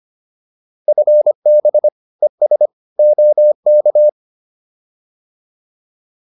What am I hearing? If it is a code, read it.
Morse code, heard as a keyed tone, about 25 wpm, 600 Hz. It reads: FB ES OK